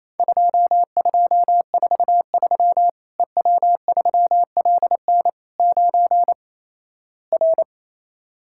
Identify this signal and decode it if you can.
Morse code, heard as a keyed tone, about 28 wpm, 700 Hz. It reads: 2243 EW3LN 9 R
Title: Worked All Europe DX Contest - QTC Exchange - 28wpm